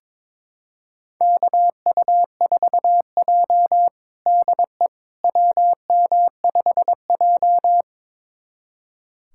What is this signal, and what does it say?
Morse code, heard as a keyed tone, about 22 wpm, 700 Hz. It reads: KU4J DE WM5J
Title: USA Callsign DE Callsign - 4 Characters - 22wpm